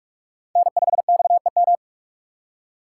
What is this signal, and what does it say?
Morse code, heard as a keyed tone, about 40 wpm, 700 Hz. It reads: N5XEM